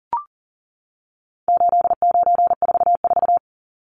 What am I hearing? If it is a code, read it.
Morse code, heard as a keyed tone, about 40 wpm, 700 Hz. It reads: 8944
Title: Numbers - 4-Digit 40wpm